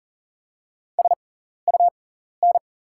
Morse code, heard as a keyed tone, about 40 wpm, spaced as for 14 wpm, 690 Hz. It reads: SUN
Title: 6x Character Spacing - Top 300 Words - 40wpm